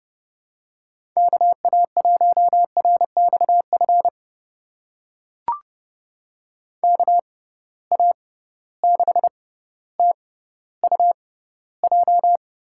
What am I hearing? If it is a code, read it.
Morse code, heard as a keyed tone, about 30 wpm, spaced as for 15 wpm, 700 Hz. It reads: KA1RXF KA6TUJ
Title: USA Call Signs - 6 Characters - 6x Character Spacing - 30wpm